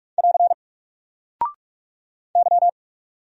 Morse code, heard as a keyed tone, about 46 wpm, 700 Hz. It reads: <AR> Y
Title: Single Character (Letter:Number:Punctuation:Prosign) Alt 45wpm